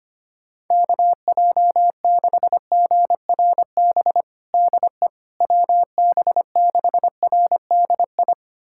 Morse code, heard as a keyed tone, about 25 wpm, 700 Hz. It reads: KJ6GRB DE WB6RDI
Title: USA Callsign DE Callsign - 6 Characters - 25wpm